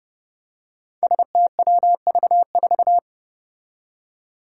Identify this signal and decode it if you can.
Morse code, heard as a keyed tone, about 30 wpm, 700 Hz. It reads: STWV4
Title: Five Challenging Letter-Number Combinations 30wpm